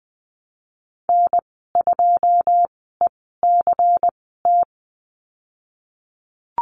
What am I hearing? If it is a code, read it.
Morse code, heard as a keyed tone, about 19 wpm, spaced as for 14 wpm, 700 Hz. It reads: N2ECT
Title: 122 Lesson 20g - Number 3 - Call Signs 20wpm